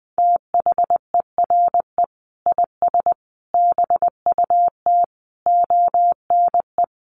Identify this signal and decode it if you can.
Morse code, heard as a keyed tone, about 20 wpm, 700 Hz. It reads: THERE IS BUT ONE
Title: Sentences from Top 100 Words 20wpm